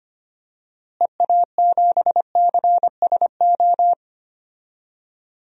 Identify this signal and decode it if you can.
Morse code, heard as a keyed tone, about 25 wpm, 700 Hz. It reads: EA7CSO
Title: DXCC-100 Call Signs - 6 Characters - Repeated 3x - 25wpm